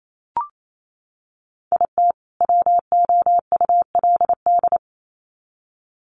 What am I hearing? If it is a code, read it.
Morse code, heard as a keyed tone, about 28 wpm, 700 Hz. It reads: IT WOULD
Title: Sets of 2 Words 28wpm